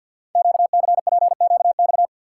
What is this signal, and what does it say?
Morse code, heard as a keyed tone, about 48 wpm, 700 Hz. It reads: QCPQX
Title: Code Groups - 5 Letters - 50wpm